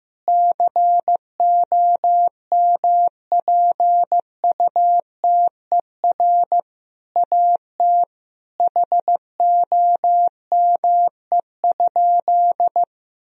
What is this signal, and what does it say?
Morse code, heard as a keyed tone, about 15 wpm, 700 Hz. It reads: COMPUTER AT HOME?
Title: Sentences from Top 1000 Words - Review Part 3 - 15wpm